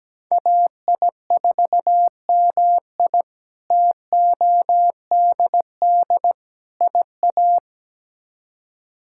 Morse code, heard as a keyed tone, about 17 wpm, 700 Hz. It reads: AI4MI TODD IA
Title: K1USN SST QSO Practice - 17wpm